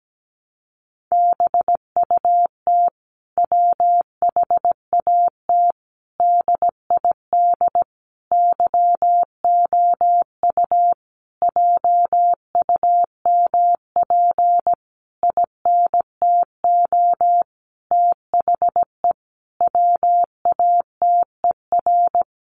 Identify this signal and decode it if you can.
Morse code, heard as a keyed tone, about 17 wpm, 700 Hz. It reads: BUT WHAT DID YOU JUMP INTO THE WATER
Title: Sentences from Top 1300 Words 17wpm